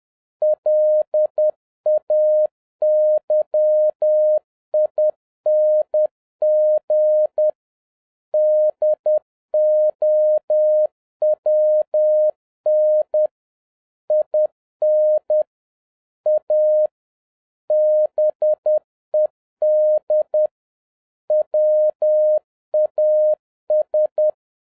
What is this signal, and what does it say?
Morse code, heard as a keyed tone, about 10 wpm, 610 Hz. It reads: LAYING DOWN IN A BED WAS
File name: 2011-07-mp3 / n56 BACK FROM A BIG ACCIDENT 10wpm